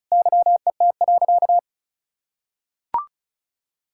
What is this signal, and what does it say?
Morse code, heard as a keyed tone, about 34 wpm, 700 Hz. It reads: YET.
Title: Sentences from Top 1600 Words 35wpm